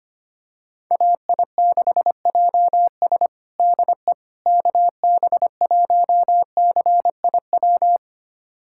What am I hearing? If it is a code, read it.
Morse code, heard as a keyed tone, about 25 wpm, 700 Hz. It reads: AI6JS DE KB1CIW
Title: USA Callsign DE Callsign 25wpm